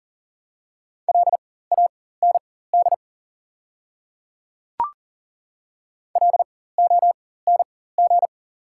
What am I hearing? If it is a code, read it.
Morse code, heard as a keyed tone, about 40 wpm, spaced as for 19 wpm, 700 Hz. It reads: LAND LONG
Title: 4x Character Spacing - Top 500 Words - 40wpm